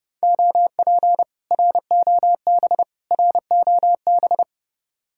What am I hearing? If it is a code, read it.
Morse code, heard as a keyed tone, about 30 wpm, 700 Hz. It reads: OP ROB ROB
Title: Speed-Racing Common QSO Phrases - 25wpm